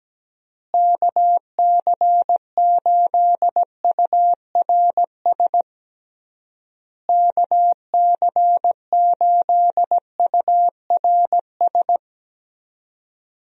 Morse code, heard as a keyed tone, about 17 wpm, 700 Hz. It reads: KC8URS KC8URS
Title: USA Call Signs - 6 Characters - Repeated 3x - 17wpm